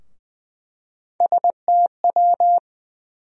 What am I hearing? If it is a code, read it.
Morse code, heard as a keyed tone, about 20 wpm, 700 Hz. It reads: STW
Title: USA Call Sign Postfixes - 3 Letters 20wpm